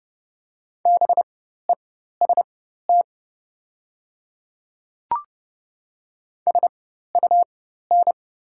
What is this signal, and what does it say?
Morse code, heard as a keyed tone, about 30 wpm, spaced as for 14 wpm, 690 Hz. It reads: BEST SUN
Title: 4x Character Spacing - Top 500 Words - 30wpm